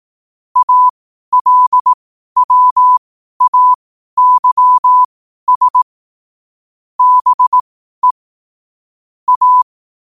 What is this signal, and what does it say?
Morse code, heard as a keyed tone, about 18 wpm, spaced as for 12 wpm, 1000 Hz. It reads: ALWAYS BE A
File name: qotdfiles / 12wpm